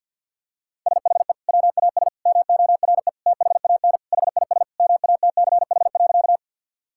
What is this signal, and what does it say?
Morse code, heard as a keyed tone, about 50 wpm, 700 Hz. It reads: SHE WAS MORE THAN HIS MATCH.
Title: Sentences from Top 1600 Words 50wpm